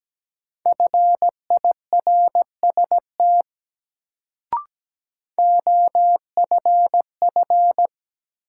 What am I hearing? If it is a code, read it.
Morse code, heard as a keyed tone, about 17 wpm, 700 Hz. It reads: FIRST OFF
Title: Sets of 2 Words - New Word Drill - 17wpm